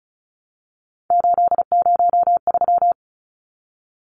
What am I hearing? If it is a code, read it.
Morse code, heard as a keyed tone, about 34 wpm, 700 Hz. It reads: 803
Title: Numbers - 3-Digit 35wpm